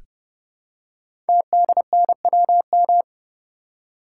Morse code, heard as a keyed tone, about 28 wpm, 700 Hz. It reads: TDNWM
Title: Code Groups - 5 Letters - 30wpm